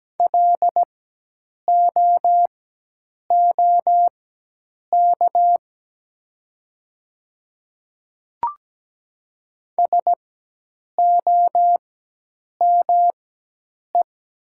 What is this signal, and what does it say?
Morse code, heard as a keyed tone, about 17 wpm, spaced as for 8 wpm, 700 Hz. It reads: LOOK SOME
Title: 4x Character Spacing - Top 100 Words - 17wpm